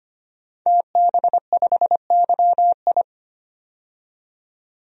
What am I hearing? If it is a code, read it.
Morse code, heard as a keyed tone, about 25 wpm, 700 Hz. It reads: TB5YI
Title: DXCC-100 Call Signs - 5 Characters - 25wpm